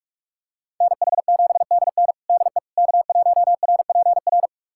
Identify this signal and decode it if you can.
Morse code, heard as a keyed tone, about 46 wpm, 700 Hz. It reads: NH7DN DE K1RPR
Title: USA Callsign DE Callsign 45wpm